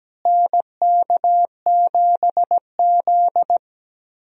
Morse code, heard as a keyed tone, about 17 wpm, 700 Hz. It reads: NK7Z
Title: USA Call Signs - 4 Characters - Repeated 3x - 17wpm